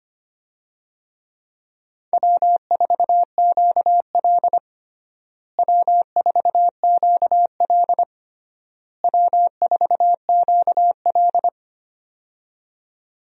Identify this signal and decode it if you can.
Morse code, heard as a keyed tone, about 25 wpm, 700 Hz. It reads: W4QL W4QL W4QL
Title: USA Call Signs - 4 Characters - Repeated 3x - 25wpm